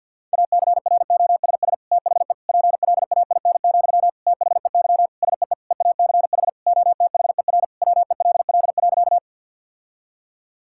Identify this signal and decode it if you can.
Morse code, heard as a keyed tone, about 52 wpm, 700 Hz. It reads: ACROSS THE PLAIN, THEY SEE EACH OTHER WELL.
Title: Sentences from Top 1000 Words - Review Part 5 - 50wpm